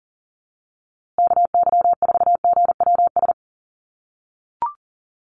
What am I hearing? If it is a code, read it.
Morse code, heard as a keyed tone, about 40 wpm, 700 Hz. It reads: KY4GWS
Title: USA Call Signs Repeated 3x 40wpm